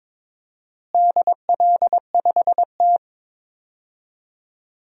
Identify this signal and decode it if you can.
Morse code, heard as a keyed tone, about 21 wpm, 700 Hz. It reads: DL5T
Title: DXCC-100 Call Signs - 4 Characters - Repeated 3x - 22wpm